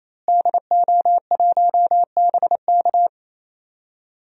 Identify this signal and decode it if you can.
Morse code, heard as a keyed tone, about 28 wpm, 690 Hz. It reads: DO1BK
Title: DXCC-100 Call Signs - 5 Characters - Repeated 3x - 28wpm